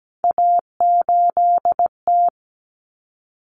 Morse code, heard as a keyed tone, about 17 wpm, 700 Hz. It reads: A8T
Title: Common Cut Numbers - 3-Digit 17wpm